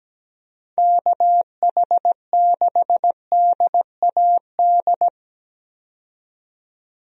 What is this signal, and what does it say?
Morse code, heard as a keyed tone, about 17 wpm, 700 Hz. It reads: KH6DAD
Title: DXCC-100 Call Signs - 6 Characters - 17wpm